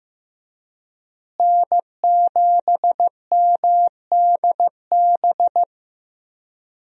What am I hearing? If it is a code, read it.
Morse code, heard as a keyed tone, about 16 wpm, 700 Hz. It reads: N7MDB
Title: USA Call Signs - 5 Characters - Repeated 3x - 15wpm